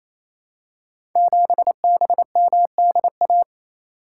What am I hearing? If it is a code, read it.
Morse code, heard as a keyed tone, about 27 wpm, 700 Hz. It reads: 7BMDA
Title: Code Groups - 5 Letter:Numbers - 28wpm